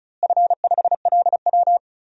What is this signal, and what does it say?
Morse code, heard as a keyed tone, about 34 wpm, 700 Hz. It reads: F5LW
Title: DXCC-100 Call Signs - 4 Characters - Repeated 3x - 35wpm